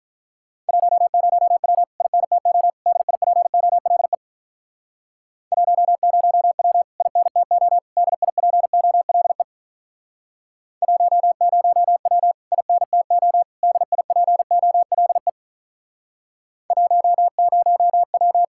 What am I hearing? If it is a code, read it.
Morse code, heard as a keyed tone, about 46 wpm, 700 Hz. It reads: 10W INTO DIPOLE 10W INTO DIPOLE 10W INTO DIPOLE 10W